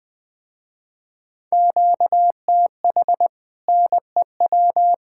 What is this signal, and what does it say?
Morse code, heard as a keyed tone, about 20 wpm, 700 Hz. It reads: QTH NEW
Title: Speed-Racing Common QSO Phrases Encore - 17wpm